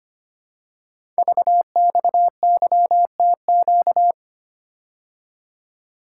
Morse code, heard as a keyed tone, about 25 wpm, 700 Hz. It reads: VXYTQ
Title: Code Groups - 5 Letter:Numbers - 25wpm